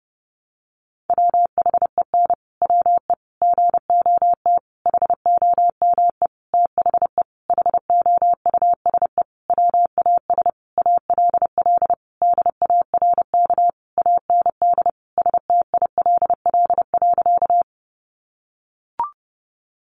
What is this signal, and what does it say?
Morse code, heard as a keyed tone, about 28 wpm, 700 Hz. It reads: WHEN WE GOT HOME THE HOUSE WAS ALL DARK AND STILL.
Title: Sentences from Top 500 Words 30wpm